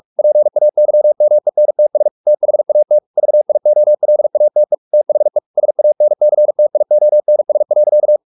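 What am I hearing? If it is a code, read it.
Morse code, heard as a keyed tone, about 44 wpm, 600 Hz. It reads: PAYMENTS THAT VIOLATE THE SANCTIONS.
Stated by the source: Morse practice at 45 wpm, the novel Never by Ken Follett in English, from CW med Gnister.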